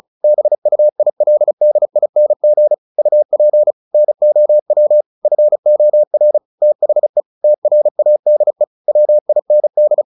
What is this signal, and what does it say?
Morse code, heard as a keyed tone, about 34 wpm, 600 Hz. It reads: BUILDING UP NOW FOR THE TRADE WIND
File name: the old man and the sea / the old man and the sea 1 35wpm